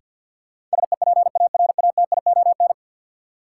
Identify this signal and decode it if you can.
Morse code, heard as a keyed tone, about 47 wpm, 700 Hz. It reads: SEPARATION
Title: Sets of 5 Words - New Word Drill - 50wpm